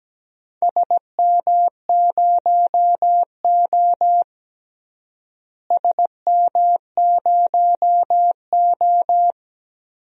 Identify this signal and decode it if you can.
Morse code, heard as a keyed tone, about 17 wpm, 700 Hz. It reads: SM0O SM0O
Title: DXCC-100 Call Signs - 4 Characters - Repeated 2x - 17wpm